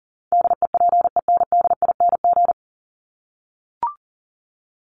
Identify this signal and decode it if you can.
Morse code, heard as a keyed tone, about 40 wpm, 700 Hz. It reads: DEPENDING
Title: ING Suffix - Top 100 Words with Base Words 40wpm